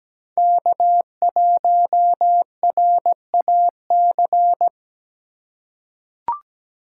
Morse code, heard as a keyed tone, about 17 wpm, 700 Hz. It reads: K1RAC
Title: USA Call Signs - 5 Characters - 3x Character Spacing - 17wpm